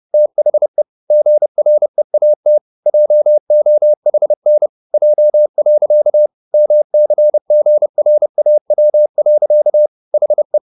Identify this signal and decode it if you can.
Morse code, heard as a keyed tone, about 30 wpm, 600 Hz. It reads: THE GREAT JOHN J. MCGRAW. HE
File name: the old man and the sea / the old man and the sea 1 30wpm